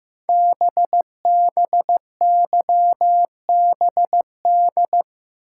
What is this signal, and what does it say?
Morse code, heard as a keyed tone, about 15 wpm, 700 Hz. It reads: BBYBD
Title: Code Groups - 5 Letters - 15wpm